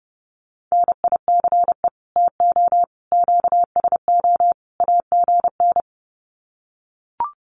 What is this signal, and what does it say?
Morse code, heard as a keyed tone, about 30 wpm, 700 Hz. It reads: NICE TO QSO AGN
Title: Common QSO Phrases 30wpm